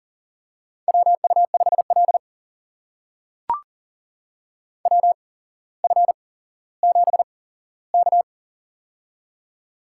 Morse code, heard as a keyed tone, about 42 wpm, spaced as for 17 wpm, 700 Hz. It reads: WU5L WF7K
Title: USA Call Signs - 4 Characters - 8x Character Spacing - 40wpm